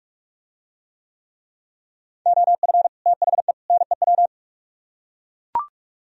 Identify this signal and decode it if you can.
Morse code, heard as a keyed tone, about 44 wpm, 700 Hz. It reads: OF THE NEW